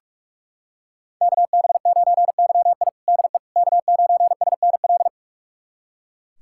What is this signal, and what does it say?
Morse code, heard as a keyed tone, about 42 wpm, 700 Hz. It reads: KB9YI DE K9SNL
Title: USA Callsign DE Callsign - 5 Characters - 45wpm